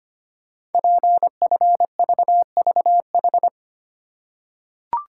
Morse code, heard as a keyed tone, about 25 wpm, 700 Hz. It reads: PFVVH
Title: Code Groups - 5 Letters - 25wpm